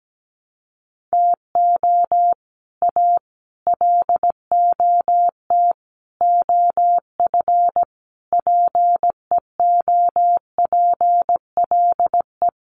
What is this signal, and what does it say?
Morse code, heard as a keyed tone, about 17 wpm, 700 Hz. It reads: TO A LOT OF PEOPLE
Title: Sets of 5 Words Encore 17wpm